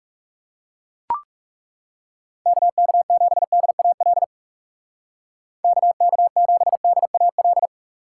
Morse code, heard as a keyed tone, about 35 wpm, 700 Hz. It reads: KK7DAL KK7DAL
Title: Speed Racing - USA 2x3 Call Signs 30wpm